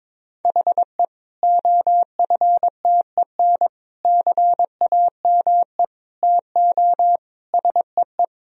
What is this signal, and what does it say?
Morse code, heard as a keyed tone, about 21 wpm, 700 Hz. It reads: HE OFTEN CAME TO SEE
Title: Sentences from Top 1000 Words - Review Part 1 - 22wpm